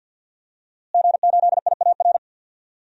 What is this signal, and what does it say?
Morse code, heard as a keyed tone, about 51 wpm, 700 Hz. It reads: G8IAR
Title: DXCC-100 Call Signs - 5 Characters - Repeated 2x - 50wpm